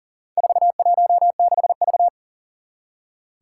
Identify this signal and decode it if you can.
Morse code, heard as a keyed tone, about 40 wpm, 700 Hz. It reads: 416V
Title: Four Challenging Letter-Number Combinations 40wpm